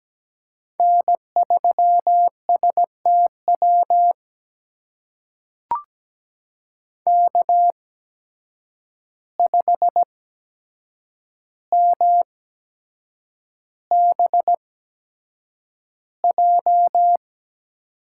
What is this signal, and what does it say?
Morse code, heard as a keyed tone, about 17 wpm, spaced as for 7 wpm, 700 Hz. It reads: N3STW K5MBJ
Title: USA Call Signs - 5 Characters - 8x Character Spacing - 17wpm